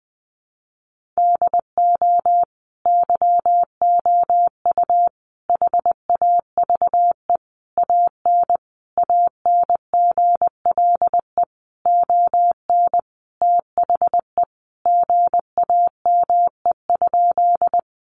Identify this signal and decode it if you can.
Morse code, heard as a keyed tone, about 20 wpm, 700 Hz. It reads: DO YOU HAVE AN ANGLE ON THE GAME?
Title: Sentences from Top 1500 Words 20wpm